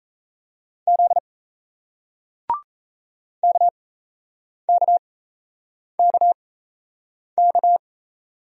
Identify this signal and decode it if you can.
Morse code, heard as a keyed tone, about 38 wpm, spaced as for 8 wpm, 700 Hz. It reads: Z KKKK